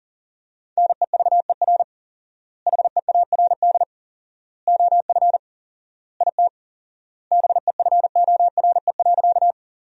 Morse code, heard as a keyed tone, about 40 wpm, 700 Hz. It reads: NEVER HEARD OF IT BEFORE.
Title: 4x Word-Spacing - Sentences from Top 400 Words - 40wpm